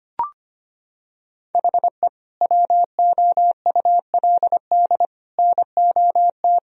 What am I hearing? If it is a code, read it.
Morse code, heard as a keyed tone, about 25 wpm, 700 Hz. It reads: HE WOULD NOT
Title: Sentences from Top 1000 Words - Review Part 2 - 25wpm